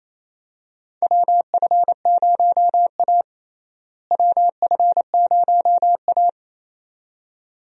Morse code, heard as a keyed tone, about 28 wpm, 700 Hz. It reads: WF0A WF0A
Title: USA Call Signs - 4 Characters - Repeated 2x - 28wpm